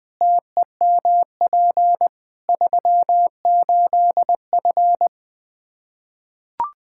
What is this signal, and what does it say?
Morse code, heard as a keyed tone, about 20 wpm, 700 Hz. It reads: TEMP 38F
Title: Practice QSO - Exchange 2 - Info - 20wpm